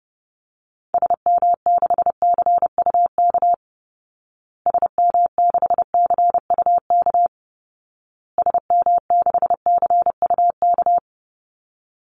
Morse code, heard as a keyed tone, about 30 wpm, 700 Hz. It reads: SM6CUK SM6CUK SM6CUK
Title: Call Signs Repeated 2x - 30wpm